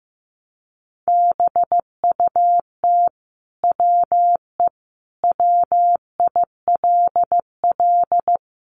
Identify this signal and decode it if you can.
Morse code, heard as a keyed tone, about 15 wpm, 700 Hz. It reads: BUT WE WILL
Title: Sentences from Top 1100 Words 15wpm